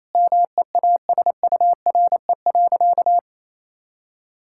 Morse code, heard as a keyed tone, about 28 wpm, 700 Hz. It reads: MEASURE.
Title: Sentences from Top 1000 Words - Review Part 6 - 28wpm